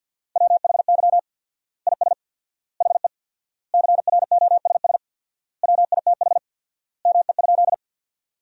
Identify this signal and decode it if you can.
Morse code, heard as a keyed tone, about 50 wpm, 700 Hz. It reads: WHY IS HE CROSS WITH ME?
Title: 4x Word-Spacing - Sentences from Top 300 Words - 50wpm